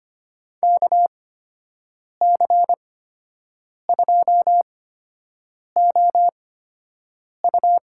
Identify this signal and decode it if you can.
Morse code, heard as a keyed tone, about 25 wpm, spaced as for 7 wpm, 700 Hz. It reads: KC2OU
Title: USA Call Signs - 8x Character Spacing - 25wpm